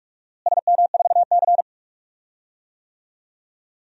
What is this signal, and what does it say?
Morse code, heard as a keyed tone, about 45 wpm, 700 Hz. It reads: SM4C
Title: DXCC-100 Call Signs - 4 Characters - Repeated 2x - 45wpm